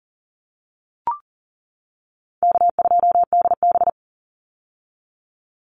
Morse code, heard as a keyed tone, about 40 wpm, 700 Hz. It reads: K2DB
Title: Easy Call Signs - 40wpm